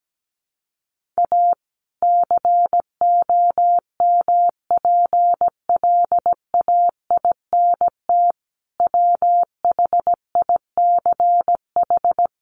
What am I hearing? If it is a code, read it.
Morse code, heard as a keyed tone, about 17 wpm, 700 Hz. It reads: A COMPLAINT WHICH
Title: Sentences from Top 1400 Words 17wpm